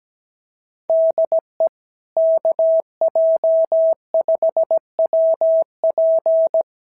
Morse code, heard as a keyed tone, about 17 wpm, 660 Hz. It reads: DE KJ5WP
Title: SOTA QSO Practice 17wpm